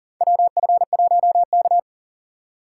Morse code, heard as a keyed tone, about 40 wpm, 700 Hz. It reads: WF1K
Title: USA Call Signs - 4 Characters - 4x Character Spacing - 40wpm